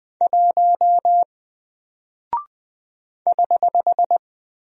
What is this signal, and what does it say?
Morse code, heard as a keyed tone, about 20 wpm, 700 Hz. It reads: 1 <HH>
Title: Single Character (Letter:Number:Punctuation:Prosign) Alt 20wpm